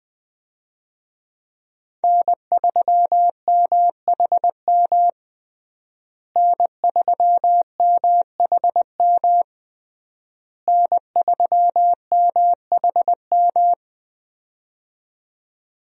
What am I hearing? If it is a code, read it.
Morse code, heard as a keyed tone, about 21 wpm, 700 Hz. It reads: N3MHM N3MHM N3MHM
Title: USA Call Signs - 5 Characters - Repeated 3x - 20wpm